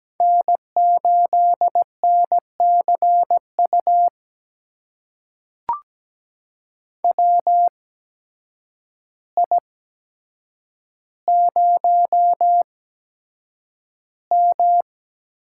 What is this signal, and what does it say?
Morse code, heard as a keyed tone, about 17 wpm, spaced as for 8 wpm, 700 Hz. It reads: N8NCU WI0M
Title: USA Call Signs - 8x Character Spacing - 17wpm